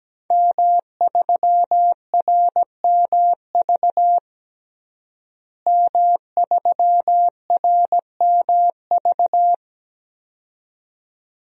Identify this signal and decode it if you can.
Morse code, heard as a keyed tone, about 17 wpm, 700 Hz. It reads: M3RMV M3RMV
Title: DXCC-100 Call Signs - 5 Characters - Repeated 2x - 17wpm